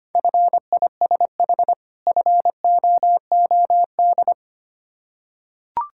Morse code, heard as a keyed tone, about 25 wpm, 700 Hz. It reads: FISH FOOD
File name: practice / 25wpm / Difficult Words with Hints - 25wpm